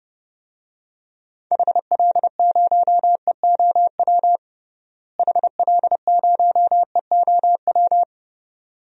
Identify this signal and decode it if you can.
Morse code, heard as a keyed tone, about 30 wpm, 700 Hz. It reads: HL0EOW HL0EOW
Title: DXCC-100 Call Signs - 6 Characters - Repeated 2x - 30wpm